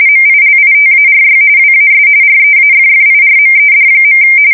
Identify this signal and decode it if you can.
RTTY, about 45.45 baud. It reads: MILITARY WAS REAFFIRMED.
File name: fsk170 / usnavy5